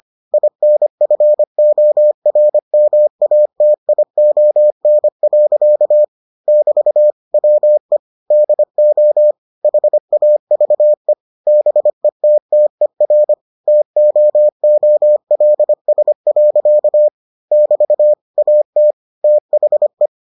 Morse code, heard as a keyed tone, about 25 wpm, 600 Hz. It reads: INFORMATION. <BT> WE DO HAVE BETTER TOOLS. <BT> AT THE